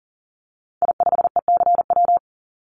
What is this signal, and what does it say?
Morse code, heard as a keyed tone, about 40 wpm, 700 Hz. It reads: I5ECW
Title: Call Signs - 40wpm